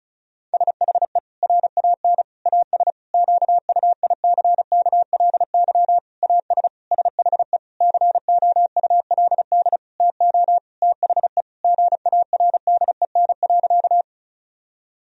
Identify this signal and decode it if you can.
Morse code, heard as a keyed tone, about 36 wpm, 700 Hz. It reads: SHE RAN AS QUICKLY AS SHE COULD TO THE GARDEN.
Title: Sentences from Top 1000 Words - Review Part 4 - 35wpm